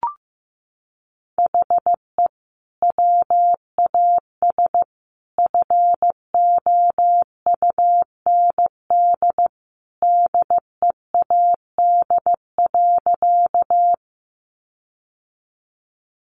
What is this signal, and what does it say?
Morse code, heard as a keyed tone, about 15 wpm, 700 Hz. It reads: HE WAS FOUND DEAD.
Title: Sentences from Top 1500 Words 15wpm